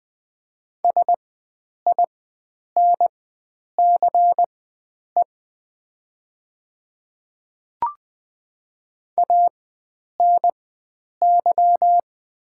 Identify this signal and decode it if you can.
Morse code, heard as a keyed tone, about 20 wpm, spaced as for 9 wpm, 700 Hz. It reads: SINCE ANY
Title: 4x Character Spacing - Top 300 Words - 20wpm